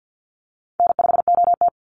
Morse code, heard as a keyed tone, about 50 wpm, 700 Hz. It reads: N5OT
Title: Easy Call Signs Repeated 2x - 50wpm